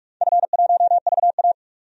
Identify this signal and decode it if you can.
Morse code, heard as a keyed tone, about 42 wpm, 700 Hz. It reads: F1VA
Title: DXCC-100 Call Signs - 4 Characters - 45wpm